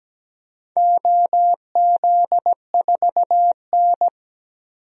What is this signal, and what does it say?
Morse code, heard as a keyed tone, about 17 wpm, 700 Hz. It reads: OZ4N
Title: DXCC-100 Call Signs - 4 Characters - 17wpm